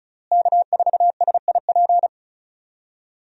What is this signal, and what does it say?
Morse code, heard as a keyed tone, about 34 wpm, 700 Hz. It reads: K4SIP
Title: USA Call Signs - 5 Characters - Repeated 2x - 35wpm